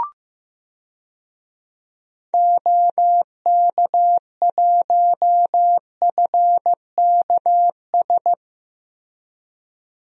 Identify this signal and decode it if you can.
Morse code, heard as a keyed tone, about 15 wpm, 700 Hz. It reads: OK1FKS
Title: DXCC-100 Call Signs - 6 Characters - Repeated 2x - 15wpm